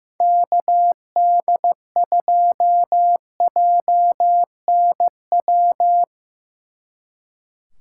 Morse code, heard as a keyed tone, about 17 wpm, 700 Hz. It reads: KD2JNW
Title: Speed Racing - USA Call Signs 15wpm